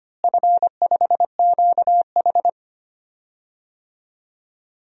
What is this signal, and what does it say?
Morse code, heard as a keyed tone, about 25 wpm, 700 Hz. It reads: F5QH
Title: DXCC-100 Call Signs - 4 Characters - Repeated 3x - 25wpm